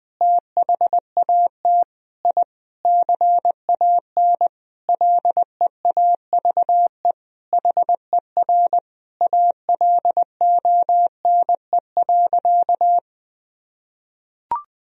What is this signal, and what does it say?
Morse code, heard as a keyed tone, about 20 wpm, 700 Hz. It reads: THAT I CAN LEAVE HER ALONE.
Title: Sentences from Top 1000 Words - Review Part 3 - 20wpm